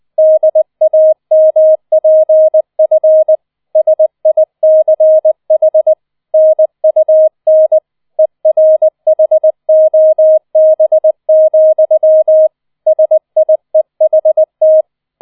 Morse code, heard as a keyed tone, about 20 wpm, 630 Hz. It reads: DAMPF SICH NUN ERHOB, SIEHT